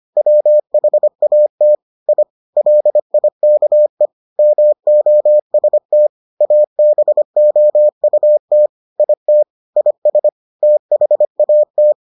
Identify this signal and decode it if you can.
Morse code, heard as a keyed tone, about 25 wpm, 600 Hz. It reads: WHAT I LIKE MOST ABOUT IT IS THAT